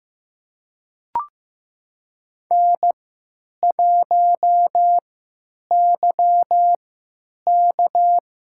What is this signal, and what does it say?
Morse code, heard as a keyed tone, about 15 wpm, spaced as for 9 wpm, 700 Hz. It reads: N1YK
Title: USA Call Signs - 4 Characters - 3x Character Spacing - 15wpm